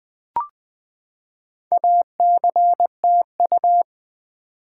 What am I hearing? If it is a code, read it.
Morse code, heard as a keyed tone, about 20 wpm, 700 Hz. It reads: ACTU
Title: Build A Word - 4 Letters At A Time - 8-Letter Words - 20wpm